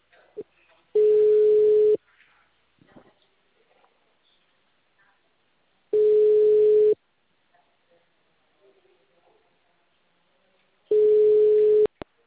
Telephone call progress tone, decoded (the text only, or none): ringback tone